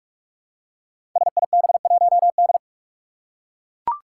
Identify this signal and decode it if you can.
Morse code, heard as a keyed tone, about 44 wpm, 700 Hz. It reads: SIB1D
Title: Five Challenging Letter-Number Combinations 45wpm